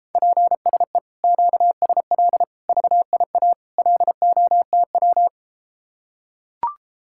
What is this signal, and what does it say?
Morse code, heard as a keyed tone, about 33 wpm, 700 Hz. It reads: PSE QSL VIA LOTW